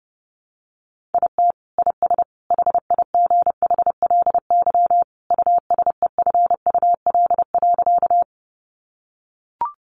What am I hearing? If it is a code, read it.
Morse code, heard as a keyed tone, about 30 wpm, 700 Hz. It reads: IT IS HIGHLY USEFUL.
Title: Sentences from Top 1100 Words 30wpm